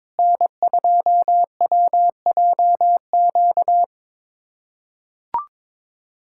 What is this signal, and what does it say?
Morse code, heard as a keyed tone, about 22 wpm, 700 Hz. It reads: N2WJQ
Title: USA Callsign DE Callsign 22wpm